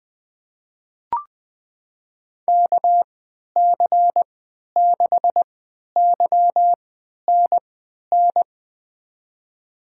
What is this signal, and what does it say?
Morse code, heard as a keyed tone, about 20 wpm, spaced as for 11 wpm, 700 Hz. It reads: KC6YNN
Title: USA Call Signs - 6 Characters - 3x Character Spacing - 20wpm